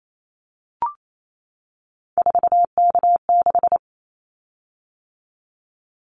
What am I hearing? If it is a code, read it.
Morse code, heard as a keyed tone, about 28 wpm, 700 Hz. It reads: 4K6
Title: Callsign Prefixes for Top 100 DXCC - 28wpm